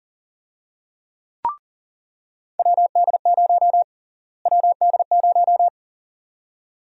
Morse code, heard as a keyed tone, about 40 wpm, 700 Hz. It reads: WD0 WD0
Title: USA Call Sign Prefixes - Repeated 2x - 40wpm